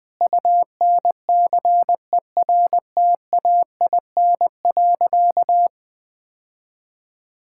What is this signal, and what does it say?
Morse code, heard as a keyed tone, about 21 wpm, 700 Hz. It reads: UNCERTAIN.
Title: Sentences with 3-Word Phrases - 20wpm